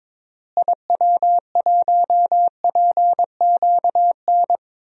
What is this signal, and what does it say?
Morse code, heard as a keyed tone, about 22 wpm, 700 Hz. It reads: IW1PQN
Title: DXCC-100 Call Signs - 6 Characters - Repeated 2x - 22wpm